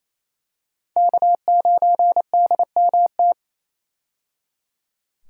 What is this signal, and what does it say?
Morse code, heard as a keyed tone, about 28 wpm, 700 Hz. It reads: K9DMT
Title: USA Call Signs - 5 Characters - 28wpm